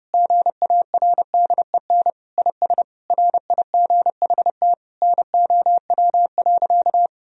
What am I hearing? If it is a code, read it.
Morse code, heard as a keyed tone, about 32 wpm, 700 Hz. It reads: GARDEN IS RIGHT NOW.
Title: Sentences from Top 1000 Words - Review Part 4 - 30wpm